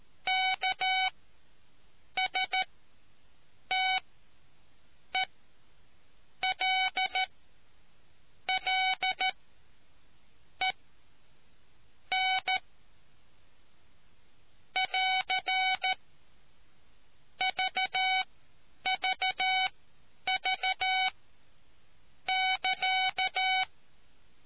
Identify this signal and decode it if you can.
Morse code, heard as a keyed tone, about 13 wpm, spaced as for 6 wpm, 750 Hz. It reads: KSTELLEN <AR> VVV <KA>